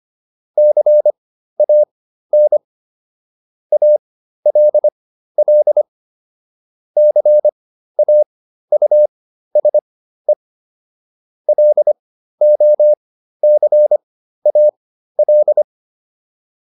Morse code, heard as a keyed tone, about 25 wpm, spaced as for 13 wpm, 600 Hz. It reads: CAN ALL CAUSE LOCAL